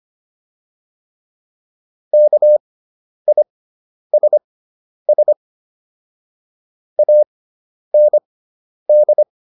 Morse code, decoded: KISS AND

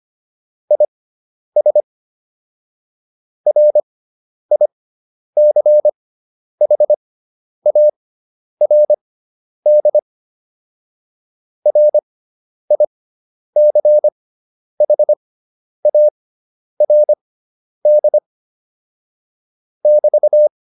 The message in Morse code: IS RICHARD RICHARD <BT>

